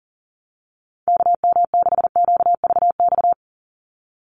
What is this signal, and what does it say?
Morse code, heard as a keyed tone, about 40 wpm, 700 Hz. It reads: KM6QVX
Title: USA Call Signs 40wpm